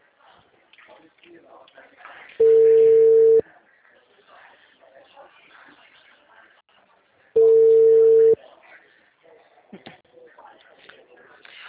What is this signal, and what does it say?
Ringback tone (China, 450 Hz), with speech.